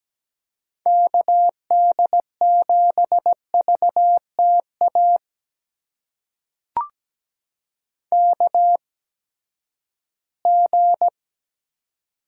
Morse code, heard as a keyed tone, about 17 wpm, spaced as for 12 wpm, 700 Hz. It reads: KD7VTA KG
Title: USA Call Signs - 6 Characters - 8x Character Spacing - 17wpm